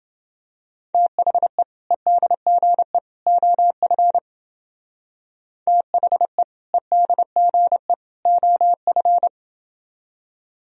Morse code, heard as a keyed tone, about 29 wpm, 700 Hz. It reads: THE EDGE OF THE EDGE OF